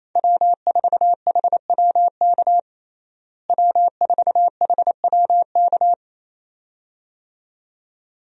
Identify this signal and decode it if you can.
Morse code, heard as a keyed tone, about 27 wpm, 700 Hz. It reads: W4HWK W4HWK
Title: USA Call Signs - 5 Characters - Repeated 2x - 28wpm